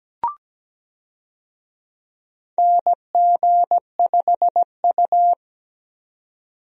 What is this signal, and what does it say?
Morse code, heard as a keyed tone, about 17 wpm, 700 Hz. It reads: NG5U
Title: USA Call Signs - 4 Characters - 17wpm